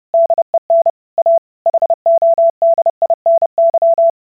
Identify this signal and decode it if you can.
Morse code, heard as a keyed tone, about 30 wpm, 670 Hz. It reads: DEN A HODINY